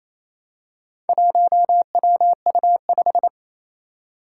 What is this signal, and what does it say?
Morse code, heard as a keyed tone, about 28 wpm, 700 Hz. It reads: 1WU5
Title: Four Challenging Letter-Number Combinations 28wpm